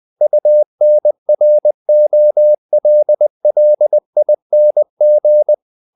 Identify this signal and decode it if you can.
Morse code, heard as a keyed tone, about 20 wpm, 600 Hz. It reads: UNROLLING
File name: the old man and the sea / the old man and the sea part 1 20wpm